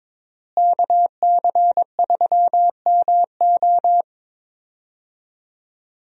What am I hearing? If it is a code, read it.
Morse code, heard as a keyed tone, about 22 wpm, 700 Hz. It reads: KC3MO
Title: USA Call Signs - 5 Characters - Repeated 3x - 22wpm